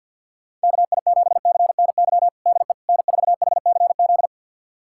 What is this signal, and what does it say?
Morse code, heard as a keyed tone, about 52 wpm, 700 Hz. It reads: KI7CNY DE N4HCZ
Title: USA Callsign DE Callsign 50wpm